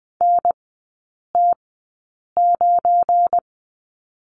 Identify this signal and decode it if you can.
Morse code, heard as a keyed tone, about 20 wpm, spaced as for 8 wpm, 700 Hz. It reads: NT9
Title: 205 Lesson 33e - Letter j - Three Characters 20wpm